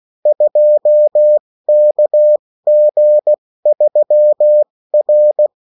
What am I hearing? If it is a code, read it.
Morse code, heard as a keyed tone, about 16 wpm, spaced as for 14 wpm, 600 Hz. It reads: 2KG3R
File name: morse mp3 / revision9-vitesse 14 mots